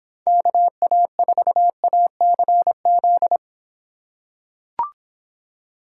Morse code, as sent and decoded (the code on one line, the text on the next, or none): -.- .- ....- .- -.-. --..
KA4ACZ